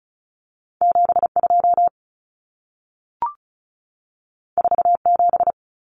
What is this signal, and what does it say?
Morse code, heard as a keyed tone, about 36 wpm, 700 Hz. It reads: 72 47
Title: Common Cut Numbers - 2-Digit 35wpm